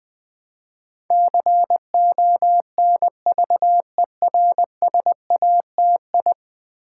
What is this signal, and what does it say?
Morse code, heard as a keyed tone, about 19 wpm, 700 Hz. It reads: CONVERSATI
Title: Build A Word - 5 Letters At A Time - 12-Letter Words - 20wpm